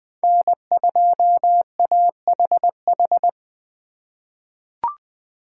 Morse code, heard as a keyed tone, about 20 wpm, 700 Hz. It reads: N2AHH
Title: USA Call Signs - 5 Characters - 3x Character Spacing - 20wpm